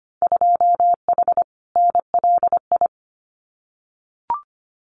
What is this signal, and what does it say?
Morse code, heard as a keyed tone, about 25 wpm, 700 Hz. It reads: 2H NLI
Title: Winter Field Day Exchanges Encore 25wpm